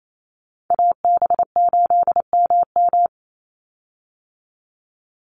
Morse code, heard as a keyed tone, about 28 wpm, 700 Hz. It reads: AB8MM
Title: USA Call Signs 28wpm